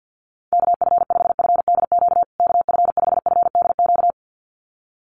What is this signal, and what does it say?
Morse code, heard as a keyed tone, about 48 wpm, 700 Hz. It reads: KF5FDQ KF5FDQ
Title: USA Call Signs Repeated 2x 50wpm